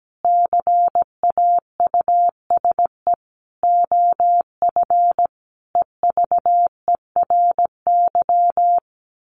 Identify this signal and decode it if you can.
Morse code, heard as a keyed tone, about 17 wpm, 700 Hz. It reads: CAUSE OF EVERY
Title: Sentences from Top 1400 Words 17wpm